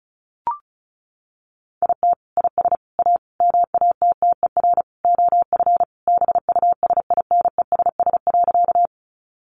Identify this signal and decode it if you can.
Morse code, heard as a keyed tone, about 34 wpm, 700 Hz. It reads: IT IS A MATTER OF BUSINESS.
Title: Sentences from Top 700 Words 35wpm